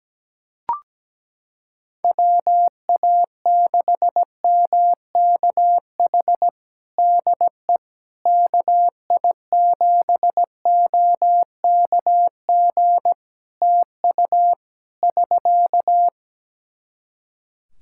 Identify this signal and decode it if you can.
Morse code, heard as a keyed tone, about 17 wpm, 700 Hz. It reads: WA6MKH DE KI7OKG TU <SK>
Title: Practice QSO - Exchange 4 - Ending - 17wpm